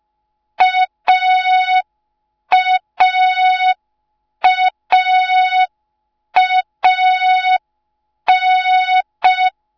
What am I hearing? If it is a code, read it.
Morse code, heard as a keyed tone, about 5 wpm, 3700 Hz. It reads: AAAAN